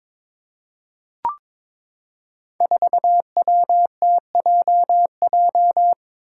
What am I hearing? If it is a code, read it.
Morse code, heard as a keyed tone, about 23 wpm, 700 Hz. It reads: 4WTJJ
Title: Five Challenging Letter-Number Combinations 22wpm